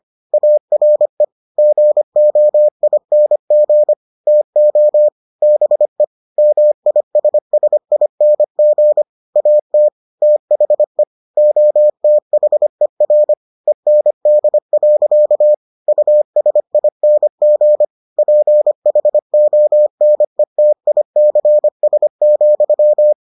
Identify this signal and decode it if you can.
Morse code, heard as a keyed tone, about 25 wpm, 600 Hz. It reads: ARE GOING TO BE MISSING AT THE OTHER END. USING PHONETICS,